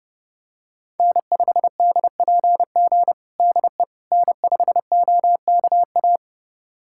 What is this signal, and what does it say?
Morse code, heard as a keyed tone, about 32 wpm, 700 Hz. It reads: N5DPG DE N5OKA
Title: USA Callsign DE Callsign - 5 Characters - 30wpm